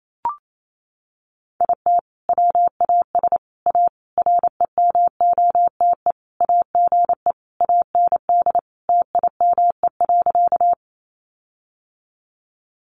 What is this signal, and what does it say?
Morse code, heard as a keyed tone, about 28 wpm, 700 Hz. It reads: IT WAS A REMOTE AGE AND TIME.
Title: Sentences from Top 1300 Words 28wpm